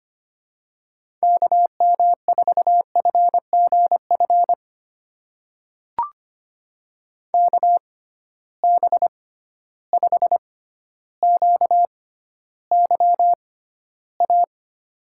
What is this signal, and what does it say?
Morse code, heard as a keyed tone, about 25 wpm, spaced as for 13 wpm, 700 Hz. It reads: KM4FGF KB5QYA
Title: USA Call Signs - 6 Characters - 6x Character Spacing - 25wpm